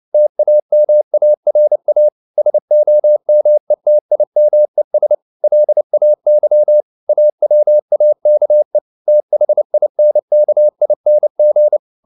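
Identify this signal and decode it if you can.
Morse code, heard as a keyed tone, about 29 wpm, 600 Hz. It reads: TAMARA SOMETIMES LAY AWAKE THINKING
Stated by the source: Morse code at 29 wpm, sending Ken Follett's novel Never in English, from CW med Gnister.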